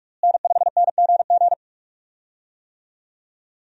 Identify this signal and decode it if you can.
Morse code, heard as a keyed tone, about 42 wpm, 700 Hz. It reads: N5NGG
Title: USA Call Signs - 5 Characters - Repeated 3x - 45wpm